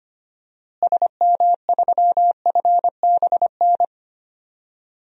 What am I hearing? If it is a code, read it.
Morse code, heard as a keyed tone, about 25 wpm, 700 Hz. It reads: SM3FBN
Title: DXCC-100 Call Signs - 6 Characters - Repeated 2x - 25wpm